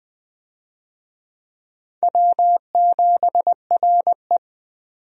Morse code, heard as a keyed tone, about 20 wpm, 700 Hz. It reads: W7RE
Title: USA Call Signs - 4 Characters - 20wpm